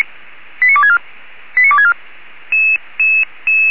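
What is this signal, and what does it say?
Five-tone selective call (ZVEI), heard as quick runs of five tones, tones of about 70 ms. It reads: 89264 89264
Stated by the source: Received by SDR; selective call numbers in /464/ folder.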